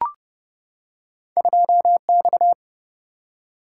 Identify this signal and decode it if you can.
Morse code, heard as a keyed tone, about 29 wpm, 700 Hz. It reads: 2X
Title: Single-Letter Single-Number 30wpm